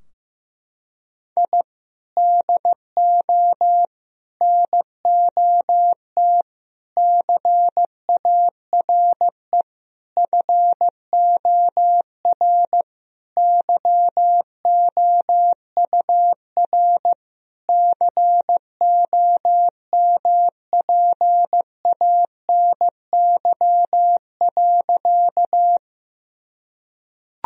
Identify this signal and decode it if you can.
Morse code, heard as a keyed tone, about 15 wpm, 700 Hz. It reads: I DO NOT CARE FOR YOUR COMPANY.
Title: Sentences from Top 1000 Words - Review Part 3 - 15wpm